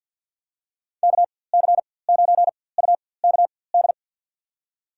Morse code, heard as a keyed tone, about 50 wpm, spaced as for 23 wpm, 700 Hz. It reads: KC9UKD
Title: USA Call Signs - 4x Character Spacing - 50wpm